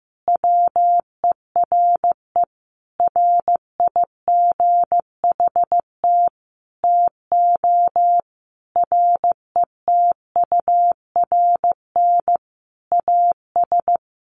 Morse code, heard as a keyed tone, about 15 wpm, 700 Hz. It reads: WERE RIGHT TO RETURN AS
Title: Sentences from Top 800 Words 15wpm